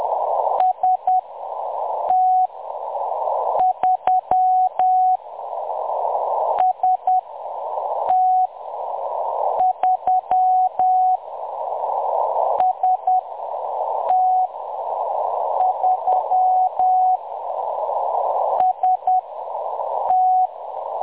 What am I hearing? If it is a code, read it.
Morse code, heard as a keyed tone, about 10 wpm, spaced as for 6 wpm, 740 Hz. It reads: ST3 ST3 ST3 ST
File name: M23-ST3- -0858utc-10184khz